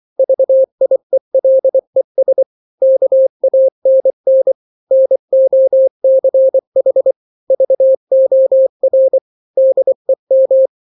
Morse code, heard as a keyed tone, about 24 wpm, 530 Hz. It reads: VIELES KANN NOCH VOR DEM
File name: daily / 23wpm